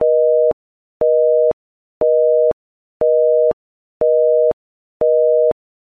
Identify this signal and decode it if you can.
Busy tone (North America), 480 + 620 Hz, with noise.
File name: BusySignalorEngagedTone